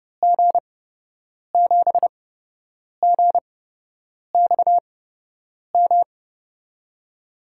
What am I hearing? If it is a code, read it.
Morse code, heard as a keyed tone, about 30 wpm, spaced as for 8 wpm, 700 Hz. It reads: G7GXM
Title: USA Call Signs - 6 Characters - 8x Character Spacing - 30wpm